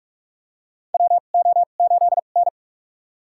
Morse code, heard as a keyed tone, about 45 wpm, spaced as for 32 wpm, 700 Hz. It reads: WO8N
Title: USA Call Signs - 2x Character Spacing - 45wpm